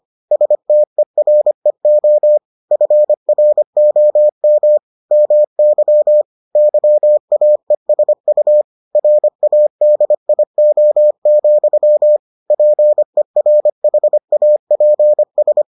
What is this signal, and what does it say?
Morse code, heard as a keyed tone, about 25 wpm, 600 Hz. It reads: STEREO FROM MY YAESU RADIO, PERHAPS